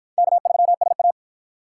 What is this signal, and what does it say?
Morse code, heard as a keyed tone, about 48 wpm, 700 Hz. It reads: K3SA